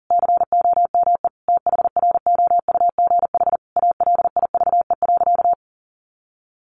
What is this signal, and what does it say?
Morse code, heard as a keyed tone, about 40 wpm, 700 Hz. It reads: COME THROUGH ALIVE.
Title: A Sound Of Thunder 40wpm